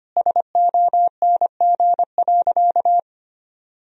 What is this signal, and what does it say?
Morse code, heard as a keyed tone, about 25 wpm, 700 Hz. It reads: SONG.
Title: Sentences from Top 1000 Words - Review Part 2 - 25wpm